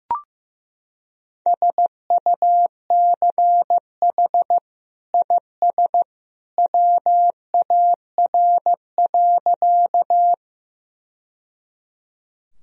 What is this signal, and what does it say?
Morse code, heard as a keyed tone, about 15 wpm, 700 Hz. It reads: SUCH IS WAR.
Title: Sentences from Top 1000 Words - Review Part 2 - 15wpm